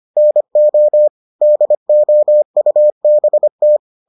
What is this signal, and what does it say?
Morse code, heard as a keyed tone, about 25 wpm, 600 Hz. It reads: NO DOUBT